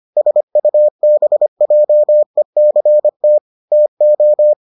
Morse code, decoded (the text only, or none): SUBJECT TO